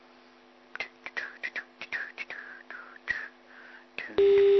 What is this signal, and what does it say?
Emergency Alert System (SAME) tones, heard as harsh buzzing bursts. It is heard from 1.1 to 3.8 s.